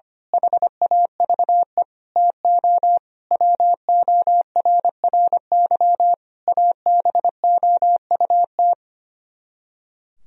Morse code, heard as a keyed tone, about 24 wpm, 700 Hz. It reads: HAVE TO WORRY ABOUT
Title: Speed-Racing - Sets of 5 Words 25wpm